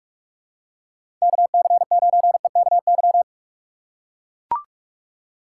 Morse code, heard as a keyed tone, about 41 wpm, 700 Hz. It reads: KC9EKY
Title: USA Call Signs - 6 Characters - 3x Character Spacing - 45wpm